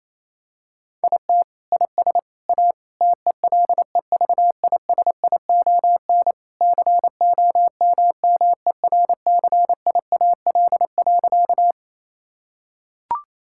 Morse code, heard as a keyed tone, about 28 wpm, 700 Hz. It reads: IT IS A TELEVISION COMMERCIAL.
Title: Sentences from Top 1000 Words - Review Part 5 - 28wpm